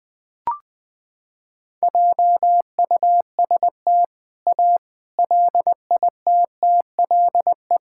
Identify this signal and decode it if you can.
Morse code, heard as a keyed tone, about 21 wpm, 700 Hz. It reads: JUST A LITTLE
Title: Sentences with 4-Word Phrases - 20wpm